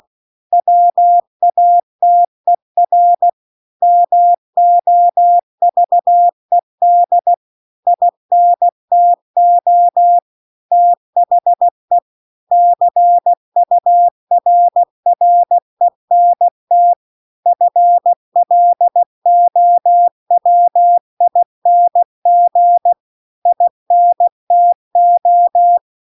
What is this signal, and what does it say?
Morse code, heard as a keyed tone, about 16 wpm, 700 Hz. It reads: WATER MOVED INTO THE CURRENT FLOWING INTO